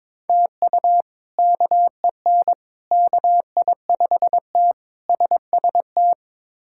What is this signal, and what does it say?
Morse code, heard as a keyed tone, about 22 wpm, 700 Hz. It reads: TU KEN KI5T SST
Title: K1USN SST QSO Practice - 22wpm